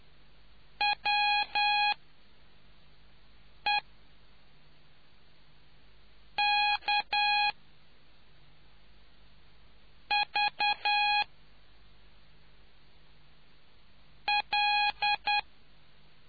Morse code, heard as a keyed tone, about 10 wpm, spaced as for 3 wpm, 830 Hz. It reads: WEKV L